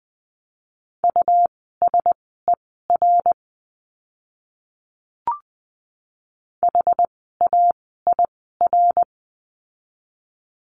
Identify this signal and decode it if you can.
Morse code, heard as a keyed tone, about 20 wpm, spaced as for 14 wpm, 700 Hz. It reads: USER HAIR
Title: 101 Lesson 17f - Letter u - Words 20wpm